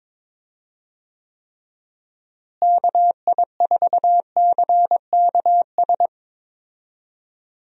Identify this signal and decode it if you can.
Morse code, heard as a keyed tone, about 22 wpm, 700 Hz. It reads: KI4CKS
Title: USA Call Signs - 6 Characters - Repeated 3x - 22wpm